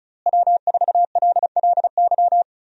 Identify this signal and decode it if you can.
Morse code, heard as a keyed tone, about 34 wpm, 700 Hz. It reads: W4LLY
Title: USA Call Signs - 5 Characters - Repeated 2x - 35wpm